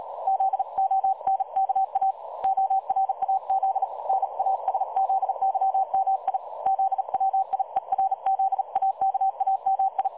Morse code, heard as a keyed tone, about 37 wpm, 770 Hz. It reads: ZONKA ORAZ NASZYMI ZWIERZAKAMI
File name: dlazaawansowanych / nagraniazpasma / websdr recording 3565.0kHz